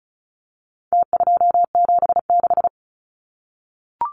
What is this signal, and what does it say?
Morse code, heard as a keyed tone, about 34 wpm, 700 Hz. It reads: T276
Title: Numbers - Contest Style - 4-Digit 35wpm